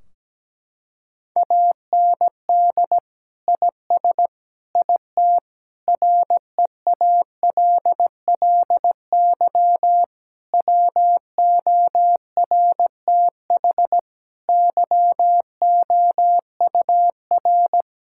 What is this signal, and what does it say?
Morse code, heard as a keyed tone, about 17 wpm, 700 Hz. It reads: AND IS IT REALLY WORTH YOUR
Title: Sentences from Top 1600 Words 17wpm